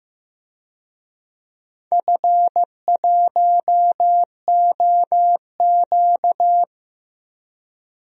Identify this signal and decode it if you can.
Morse code, heard as a keyed tone, about 15 wpm, 700 Hz. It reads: F1OQ
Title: DXCC-100 Call Signs - 4 Characters - 15wpm